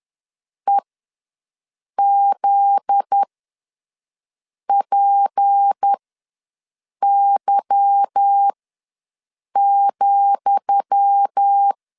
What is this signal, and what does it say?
Morse code, heard as a keyed tone, about 10 wpm, spaced as for 5 wpm, 800 Hz. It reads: EZPY,
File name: LEZIONI / 10 SIGNS / 10D COMA ws 10wpm 4314